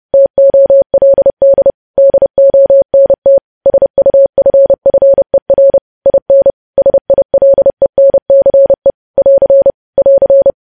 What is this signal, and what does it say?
Morse code, heard as a keyed tone, about 30 wpm, 550 Hz. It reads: TOLD DONT SUFFER IN SILENCE <AR> <AR>
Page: news headlines in morse code at 30 wpm